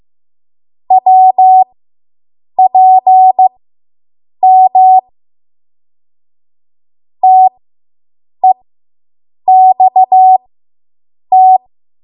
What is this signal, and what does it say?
Morse code, heard as a keyed tone, about 15 wpm, spaced as for 7 wpm, 750 Hz. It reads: WPM TEXT